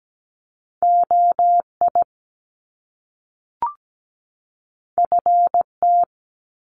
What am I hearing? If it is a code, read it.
Morse code, heard as a keyed tone, about 17 wpm, 700 Hz. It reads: OI FT
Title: Top Two Letter Combinations in Words 17wpm